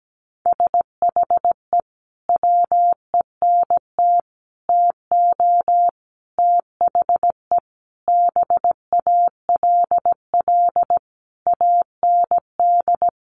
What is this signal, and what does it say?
Morse code, heard as a keyed tone, about 17 wpm, 700 Hz. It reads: SHE WENT TO THE BALL AND
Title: Sentences from Top 900 Words 17wpm